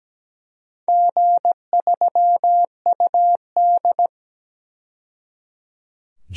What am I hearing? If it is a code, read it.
Morse code, heard as a keyed tone, about 17 wpm, 700 Hz. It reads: G3UD
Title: DXCC-100 Call Signs - 4 Characters - 17wpm